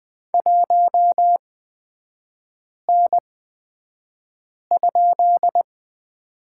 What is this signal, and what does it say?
Morse code, heard as a keyed tone, about 20 wpm, spaced as for 5 wpm, 700 Hz. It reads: 1N?
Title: 250 Lesson 40e - BK - Three Characters 20wpm